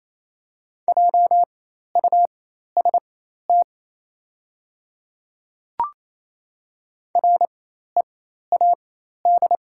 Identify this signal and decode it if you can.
Morse code, heard as a keyed tone, about 28 wpm, spaced as for 13 wpm, 700 Hz. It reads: JUST READ
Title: 4x Character Spacing - Top 200 Words - 28wpm